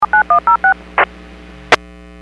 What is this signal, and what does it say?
DTMF, heard as telephone touch tones, tones of about 110 ms. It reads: 79106
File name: WPS / 04322 21-17 WPS 451.25 WPS